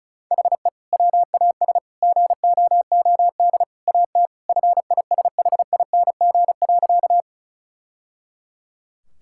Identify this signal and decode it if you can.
Morse code, heard as a keyed tone, about 34 wpm, 700 Hz. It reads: HE WAS GOOD AT FISHING.
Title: Sentences from Top 1000 Words - Review Part 5 - 35wpm